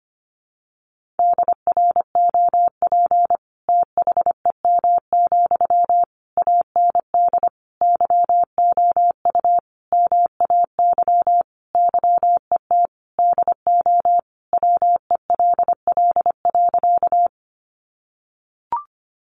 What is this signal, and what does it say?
Morse code, heard as a keyed tone, about 25 wpm, 700 Hz. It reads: DROP THEM, AND YOU MAY YET DO WELL.
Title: Sentences from Top 1400 Words 25wpm